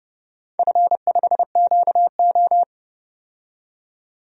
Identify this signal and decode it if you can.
Morse code, heard as a keyed tone, about 30 wpm, 700 Hz. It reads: F5QO
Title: DXCC-100 Call Signs - 4 Characters - 30wpm